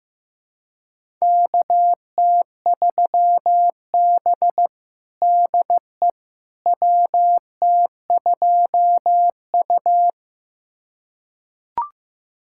Morse code, heard as a keyed tone, about 15 wpm, 700 Hz. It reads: KT3B DE WT2U
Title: USA Callsign DE Callsign - 4 Characters - 15wpm